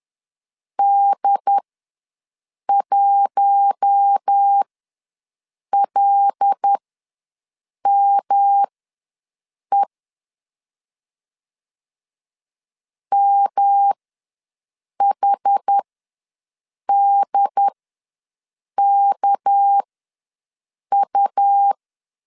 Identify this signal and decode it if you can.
Morse code, heard as a keyed tone, about 10 wpm, spaced as for 6 wpm, 800 Hz. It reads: D1LME MHDKU
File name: LEZIONI / 6 PKX0 / 6 PKX0 ws 10wpm 4220